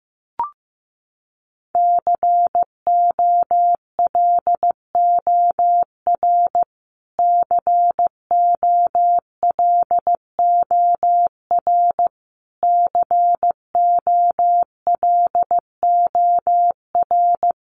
Morse code, heard as a keyed tone, about 15 wpm, 700 Hz. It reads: COLOR COLOR COLOR
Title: Top 500 Words Repeated 15wpm